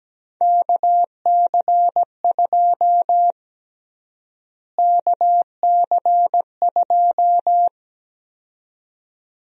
Morse code, heard as a keyed tone, about 17 wpm, 700 Hz. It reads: KC2 KC2
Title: USA Call Sign Prefixes - Repeated 2x - 17wpm